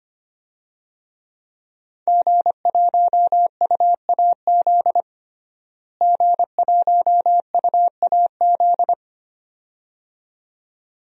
Morse code, heard as a keyed tone, about 25 wpm, 700 Hz. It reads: G1UAZ G1UAZ
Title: DXCC-100 Call Signs - 5 Characters - Repeated 2x - 25wpm